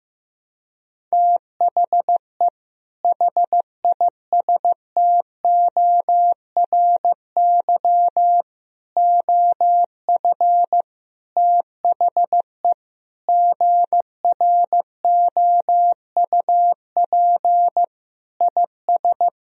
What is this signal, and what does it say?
Morse code, heard as a keyed tone, about 15 wpm, 700 Hz. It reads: THE HISTORY OF THE GROUP IS
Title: Sentences from Top 1000 Words - Review Part 1 - 15wpm